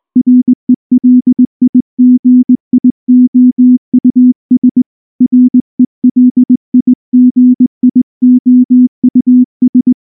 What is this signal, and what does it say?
Morse code, heard as a keyed tone, about 19 wpm, 250 Hz. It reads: RELIGIOUS RELIGIOUS
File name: morse / morse-training-20WPM-7